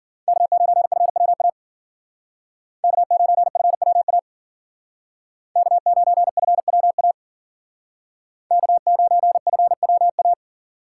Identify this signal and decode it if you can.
Morse code, heard as a keyed tone, about 51 wpm, 700 Hz. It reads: K9FWA K9FWA K9FWA K9FWA